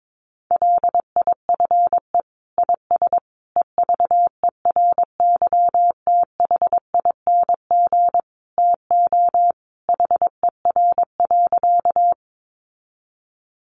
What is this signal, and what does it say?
Morse code, heard as a keyed tone, about 22 wpm, 700 Hz. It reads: LIFE IS EVERYTHING TO HER.
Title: Sentences from Top 700 Words 22wpm